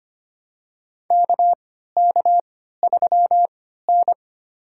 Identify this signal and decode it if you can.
Morse code, heard as a keyed tone, about 25 wpm, spaced as for 14 wpm, 700 Hz. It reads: KK3N
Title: USA Call Signs - 4 Characters - 3x Character Spacing - 25wpm